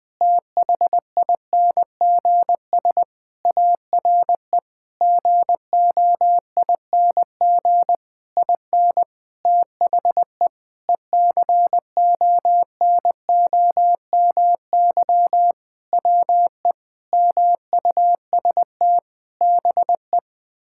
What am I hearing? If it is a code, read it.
Morse code, heard as a keyed tone, about 20 wpm, 700 Hz. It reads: THINGS ARE GOING IN THE ECONOMY WE MUST BE
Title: Sentences with 8-Word Phrases - 20wpm